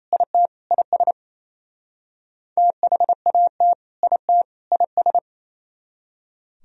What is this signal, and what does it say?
Morse code, heard as a keyed tone, about 30 wpm, 700 Hz. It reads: IT IS THAT IT IS
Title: Speed-Racing - Sets of 3 Words 28wpm